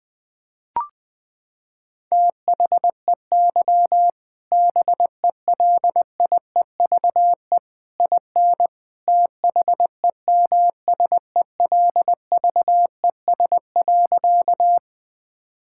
Morse code, heard as a keyed tone, about 20 wpm, 700 Hz. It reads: THEY BELIEVE IN THEMSELVES.
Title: Sentences from Top 1000 Words - Review Part 5 - 20wpm